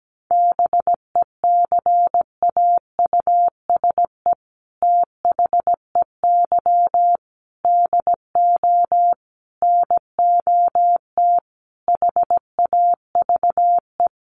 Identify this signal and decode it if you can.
Morse code, heard as a keyed tone, about 17 wpm, 700 Hz. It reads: BECAUSE THEY DO NOT HAVE
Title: Sets of 5 Words Encore2 17wpm